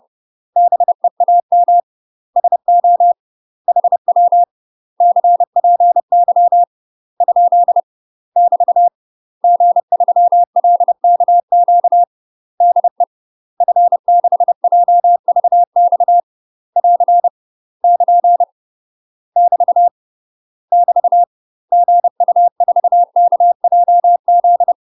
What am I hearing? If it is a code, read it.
Morse code, heard as a keyed tone, about 30 wpm, 700 Hz. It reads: BEAM SO HW CPY ? <BT> G3LKQ DE F6JVX <AR> <KN> <BT> <BT> GU4KJZ
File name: UK MASTER QSOs TRAINING / 30WPM / UK MASTER QSOs 0024